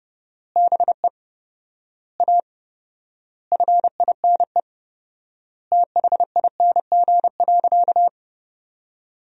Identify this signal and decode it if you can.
Morse code, heard as a keyed tone, about 30 wpm, 700 Hz. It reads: BE A FINE THING.
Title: 4x Word-Spacing - Sentences from Top 500 Words - 30wpm